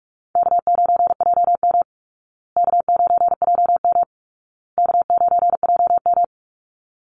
Morse code, heard as a keyed tone, about 46 wpm, 700 Hz. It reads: K9JM K9JM K9JM
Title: Easy Call Signs Repeated 3x - 45wpm